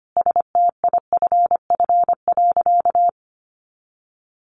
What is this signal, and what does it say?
Morse code, heard as a keyed tone, about 25 wpm, 700 Hz. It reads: STIFF.
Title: A Sound Of Thunder 25wpm